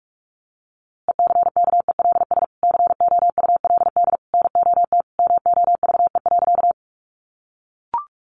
Morse code, heard as a keyed tone, about 44 wpm, 700 Hz. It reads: ECKELS COULD NOT MOVE.
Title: A Sound Of Thunder 45wpm